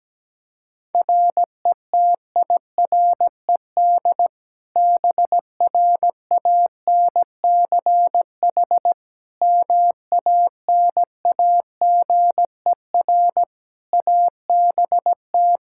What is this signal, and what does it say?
Morse code, heard as a keyed tone, about 17 wpm, 700 Hz. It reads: RETIRED BRANCH MANAGER ABT
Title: Practice QSO - Exchange 3 - More Info - 17wpm